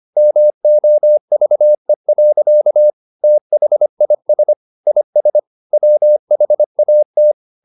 Morse code, heard as a keyed tone, about 25 wpm, 600 Hz. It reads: MOVE. THIS IS WHAT